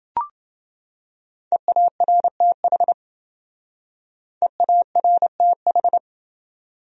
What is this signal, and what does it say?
Morse code, heard as a keyed tone, about 28 wpm, 700 Hz. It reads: EARTH EARTH